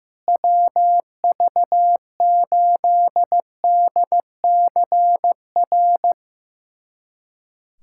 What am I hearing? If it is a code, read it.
Morse code, heard as a keyed tone, about 15 wpm, 700 Hz. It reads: WV8DCR
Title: USA Callsign DE Callsign 15wpm